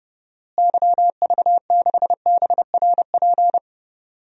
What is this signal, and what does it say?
Morse code, heard as a keyed tone, about 32 wpm, 700 Hz. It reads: YV6BRP
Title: DXCC-100 Call Signs - 6 Characters - Repeated 2x - 30wpm